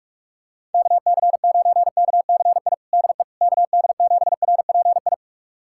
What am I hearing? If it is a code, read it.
Morse code, heard as a keyed tone, about 46 wpm, 700 Hz. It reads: KC9KCI DE KD7RPI
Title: USA Callsign DE Callsign 45wpm